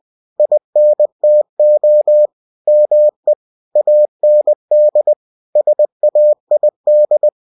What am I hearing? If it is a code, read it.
Morse code, heard as a keyed tone, about 20 wpm, 600 Hz. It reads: INTO ME AND SAID
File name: cw-books / winnie-the-pooh / winnie-20wpm-page-0109